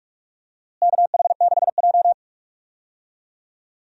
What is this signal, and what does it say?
Morse code, heard as a keyed tone, about 46 wpm, 700 Hz. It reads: KH6J
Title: DXCC-100 Call Signs - 4 Characters - Repeated 2x - 45wpm